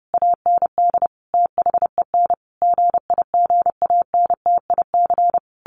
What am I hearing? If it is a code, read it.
Morse code, heard as a keyed tone, about 30 wpm, 700 Hz. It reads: AND THEN GIGANTIC
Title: A Sound Of Thunder 30wpm